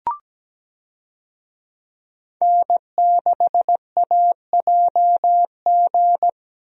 Morse code, heard as a keyed tone, about 17 wpm, 700 Hz. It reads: N6AJG
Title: USA Call Signs - 5 Characters - Repeated 2x - 17wpm